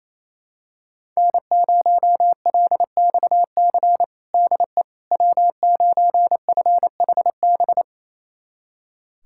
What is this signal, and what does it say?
Morse code, heard as a keyed tone, about 28 wpm, 700 Hz. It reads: N0LXC DE W9FHB
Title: USA Callsign DE Callsign - 5 Characters - 28wpm